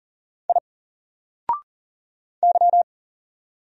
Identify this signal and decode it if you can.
Morse code, heard as a keyed tone, about 40 wpm, 700 Hz. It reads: I Y
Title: Single Character (Letter:Number:Punctuation:Prosign) 40wpm